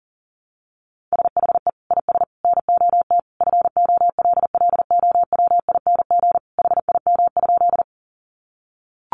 Morse code, heard as a keyed tone, about 40 wpm, 700 Hz. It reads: SHE IS NOT FOLLOWING HIM?
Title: Sentences from Top 700 Words 40wpm